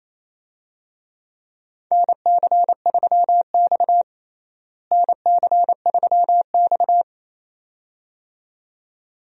Morse code, heard as a keyed tone, about 27 wpm, 700 Hz. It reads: NC3X NC3X
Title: USA Call Signs - 4 Characters - Repeated 2x - 28wpm